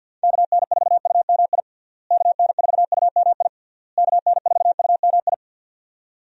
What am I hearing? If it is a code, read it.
Morse code, heard as a keyed tone, about 49 wpm, 700 Hz. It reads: KN4UMI KN4UMI KN4UMI
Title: USA Call Signs - 6 Characters - Repeated 3x - 50wpm